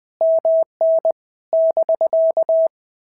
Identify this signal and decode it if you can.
Morse code, heard as a keyed tone, about 20 wpm, 650 Hz. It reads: MN <BK>
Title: POTA QSO Practice - 20wpm